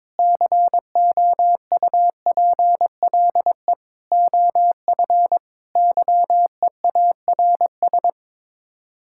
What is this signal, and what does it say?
Morse code, heard as a keyed tone, about 22 wpm, 700 Hz. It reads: COUPLE OF YEARS
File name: practice / 22wpm / Sets of 6 Words - 22wpm